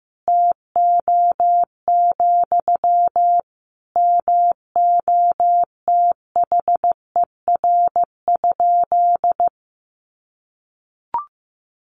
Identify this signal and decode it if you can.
Morse code, heard as a keyed tone, about 14 wpm, 700 Hz. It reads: TO, MOTHER?
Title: Sentences from Top 1000 Words 15wpm